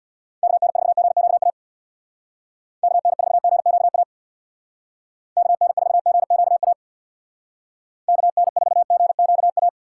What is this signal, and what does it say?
Morse code, heard as a keyed tone, about 66 wpm, 700 Hz. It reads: KN4GQA KN4GQA KN4GQA KN4GQA